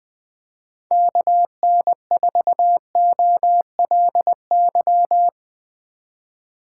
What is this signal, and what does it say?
Morse code, heard as a keyed tone, about 20 wpm, 700 Hz. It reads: KN4OLY